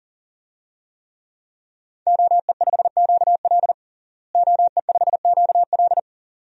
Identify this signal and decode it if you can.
Morse code, heard as a keyed tone, about 37 wpm, 700 Hz. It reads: OE5QL OE5QL
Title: DXCC-100 Call Signs - 5 Characters - Repeated 2x - 40wpm